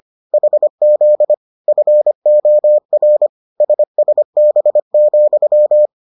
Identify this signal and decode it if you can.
Morse code, heard as a keyed tone, about 25 wpm, 600 Hz. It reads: HZ FOR SSB,